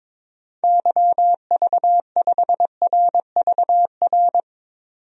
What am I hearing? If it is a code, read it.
Morse code, heard as a keyed tone, about 22 wpm, 700 Hz. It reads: YV5RVR
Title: DXCC-100 Call Signs - 6 Characters - 22wpm